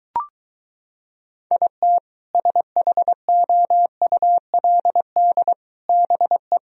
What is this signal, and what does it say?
Morse code, heard as a keyed tone, about 23 wpm, 700 Hz. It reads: IT SHOULD BE